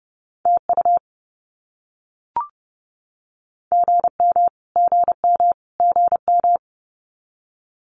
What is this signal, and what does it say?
Morse code, heard as a keyed tone, about 30 wpm, 700 Hz. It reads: TU GM GM GM
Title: Abbreviations Repeated 3x - 30wpm